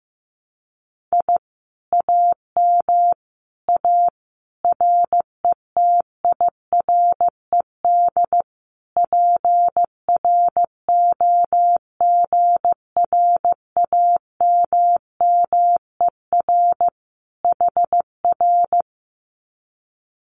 Morse code, heard as a keyed tone, about 15 wpm, 700 Hz. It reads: I AM A RETIRED PROGRAMMER HR
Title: Common QSO Phrases Encore 15wpm